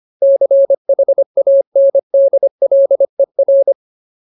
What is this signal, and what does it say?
Morse code, heard as a keyed tone, about 25 wpm, 550 Hz. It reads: CHANDLER
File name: Ultimate Word List / 25WPM Ultimate